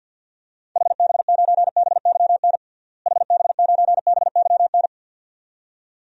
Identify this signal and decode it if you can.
Morse code, heard as a keyed tone, about 47 wpm, 700 Hz. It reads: HB9BYN HB9BYN
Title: DXCC-100 Call Signs - 6 Characters - Repeated 2x - 50wpm